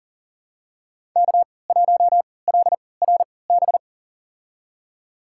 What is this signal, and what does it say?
Morse code, heard as a keyed tone, about 38 wpm, spaced as for 22 wpm, 700 Hz. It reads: K1LRB
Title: USA Call Signs - 5 Characters - 3x Character Spacing - 40wpm